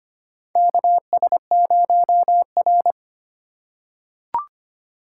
Morse code, heard as a keyed tone, about 25 wpm, 700 Hz. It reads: KS0R
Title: USA Call Signs - 4 Characters - 4x Character Spacing - 25wpm